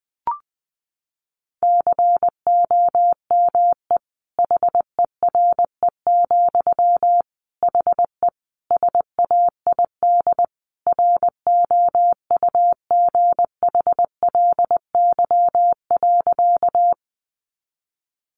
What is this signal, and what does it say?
Morse code, heard as a keyed tone, about 20 wpm, 700 Hz. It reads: COME HERE, HE SAID ROUGHLY.
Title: Sentences from Top 1500 Words 20wpm